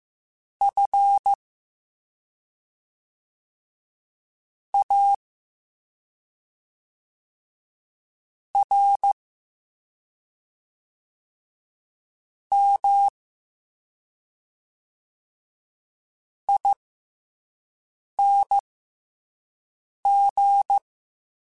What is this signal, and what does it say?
Morse code, heard as a keyed tone, about 15 wpm, spaced as for 3 wpm, 800 Hz. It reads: FARMING